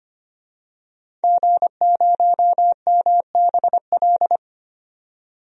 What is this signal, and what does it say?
Morse code, heard as a keyed tone, about 25 wpm, 700 Hz. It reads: G0MBL
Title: DXCC-100 Call Signs - 5 Characters - 25wpm